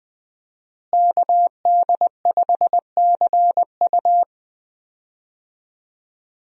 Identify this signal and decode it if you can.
Morse code, heard as a keyed tone, about 21 wpm, 700 Hz. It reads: KD5CU
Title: USA Call Signs - 5 Characters - Repeated 3x - 20wpm